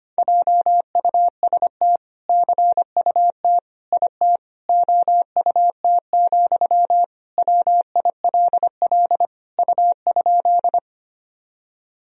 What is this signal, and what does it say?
Morse code, heard as a keyed tone, about 25 wpm, 700 Hz. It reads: JUST CUT IT OUT, WILL U?
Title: Sentences from Top 300 Words and Abbreviations 25wpm